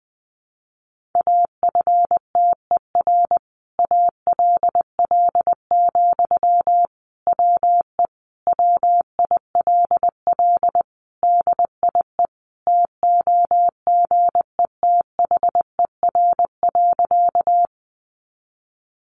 Morse code, heard as a keyed tone, about 20 wpm, 700 Hz. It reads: AFTER ALL, WE WILL DIE TOGETHER.
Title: Sentences from Top 1300 Words 20wpm